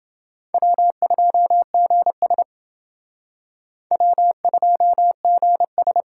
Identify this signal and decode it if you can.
Morse code, heard as a keyed tone, about 30 wpm, 700 Hz. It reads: W2GS W2GS